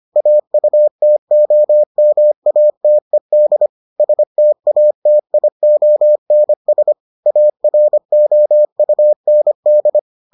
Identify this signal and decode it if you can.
Morse code, heard as a keyed tone, about 25 wpm, 600 Hz. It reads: AUTOMATED STATIONS AROUND